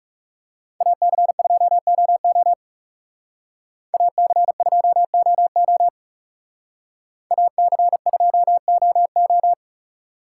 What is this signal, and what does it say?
Morse code, heard as a keyed tone, about 35 wpm, 700 Hz. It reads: AC2OO AC2OO AC2OO
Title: Speed Racing - USA 2x2 Call Signs 30wpm